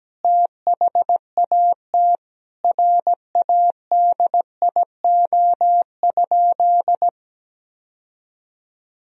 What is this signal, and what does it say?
Morse code, heard as a keyed tone, about 17 wpm, 700 Hz. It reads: THAT RADIO?
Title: Sentences from Top 1000 Words - Review Part 3 - 17wpm